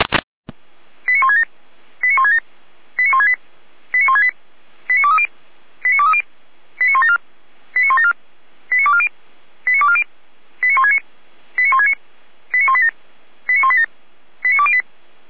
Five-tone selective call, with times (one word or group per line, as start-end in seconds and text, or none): 1.1-1.4 89167
2.0-2.4 89167
3.0-3.3 89168
3.9-4.3 89168
4.9-5.3 89230
5.8-6.2 89230
6.8-7.2 89174
7.8-8.1 89174
8.7-9.1 89240
9.7-10.0 89240
10.6-11.0 89169
11.6-11.9 89169
12.5-12.9 89187
13.5-13.8 89187
14.4-14.8 89208